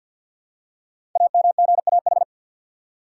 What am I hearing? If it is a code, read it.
Morse code, heard as a keyed tone, about 48 wpm, 690 Hz. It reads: AMGAH
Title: Code Groups - 5 Letters - 50wpm